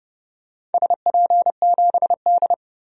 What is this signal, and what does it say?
Morse code, heard as a keyed tone, about 32 wpm, 700 Hz. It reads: SP7D
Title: DXCC-100 Call Signs - 4 Characters - Repeated 2x - 30wpm